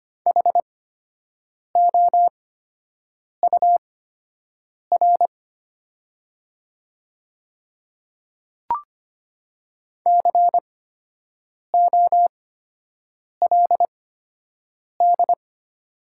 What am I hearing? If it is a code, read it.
Morse code, heard as a keyed tone, about 25 wpm, spaced as for 7 wpm, 700 Hz. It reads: HOUR COLD
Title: 8x Character Spacing - Top 500 Words - 25wpm